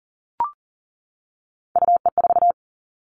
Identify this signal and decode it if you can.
Morse code, heard as a keyed tone, about 42 wpm, 700 Hz. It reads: UE4
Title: Three Challenging Letter-Number Combinations 40wpm